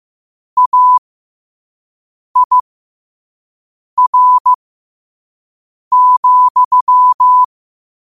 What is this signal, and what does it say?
Morse code, heard as a keyed tone, about 15 wpm, spaced as for 5 wpm, 1000 Hz. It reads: AIR,
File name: qotdfiles / 5wpm